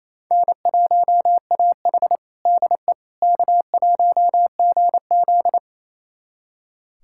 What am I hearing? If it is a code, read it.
Morse code, heard as a keyed tone, about 28 wpm, 700 Hz. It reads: N1AH DE K1GZ
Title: USA Callsign DE Callsign - 4 Characters - 28wpm